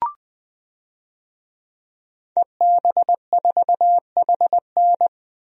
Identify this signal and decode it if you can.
Morse code, heard as a keyed tone, about 21 wpm, 700 Hz. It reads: EB4HN
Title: DXCC-100 Call Signs - 5 Characters - 20wpm